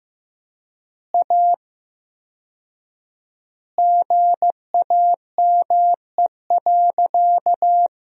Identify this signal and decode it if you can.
Morse code, heard as a keyed tone, about 15 wpm, 700 Hz. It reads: A GAME.
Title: 4x Word-Spacing - Sentences from Top 500 Words - 15wpm